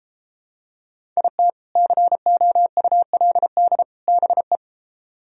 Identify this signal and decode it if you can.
Morse code, heard as a keyed tone, about 33 wpm, 700 Hz. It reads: IT COULD BE